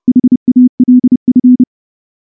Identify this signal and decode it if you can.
Morse code, heard as a keyed tone, about 28 wpm, 250 Hz. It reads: HALF
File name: morse-training-30WPM